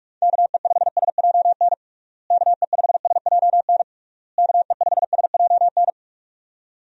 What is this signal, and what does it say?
Morse code, heard as a keyed tone, about 46 wpm, 700 Hz. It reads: KE5SJN KE5SJN KE5SJN
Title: USA Call Signs - 6 Characters - Repeated 3x - 45wpm